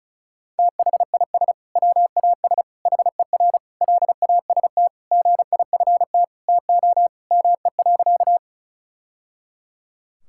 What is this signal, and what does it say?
Morse code, heard as a keyed tone, about 34 wpm, 700 Hz. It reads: THIS WAS HER LAST GIFT TO ME.
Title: Sentences from Top 1000 Words - Review Part 6 - 35wpm